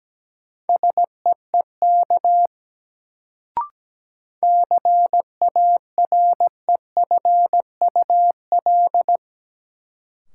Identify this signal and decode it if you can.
Morse code, heard as a keyed tone, about 17 wpm, 700 Hz. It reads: SEEK CAREFUL
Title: Sentences from Top 1600 Words - New Word Drill Alt - 17wpm